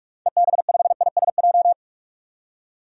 Morse code, heard as a keyed tone, about 42 wpm, 700 Hz. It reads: EB5ISJ
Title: DXCC-100 Call Signs - 6 Characters - 45wpm